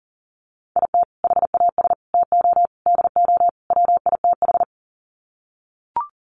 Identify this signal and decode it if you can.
Morse code, heard as a keyed tone, about 40 wpm, 700 Hz. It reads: IT HAS TO DO WITH
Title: Sets of 5 Words 40wpm